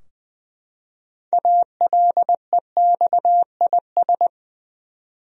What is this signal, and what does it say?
Morse code, heard as a keyed tone, about 20 wpm, 700 Hz. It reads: ALEXIS
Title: USA Common Names - 20wpm